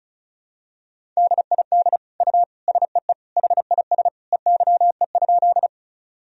Morse code, heard as a keyed tone, about 36 wpm, 700 Hz. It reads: DID U SEE HIS EYE?
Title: Sentences from Top 300 Words and Abbreviations 35wpm